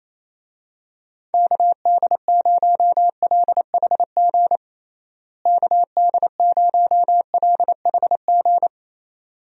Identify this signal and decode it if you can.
Morse code, heard as a keyed tone, about 28 wpm, 700 Hz. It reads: KD0LHG KD0LHG
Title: USA Call Signs - 6 Characters - Repeated 2x - 28wpm